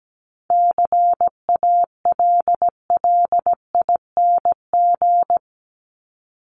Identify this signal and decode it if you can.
Morse code, heard as a keyed tone, about 17 wpm, 700 Hz. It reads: CALLING
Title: ING Suffix - Top 100 Words with Base Words 17wpm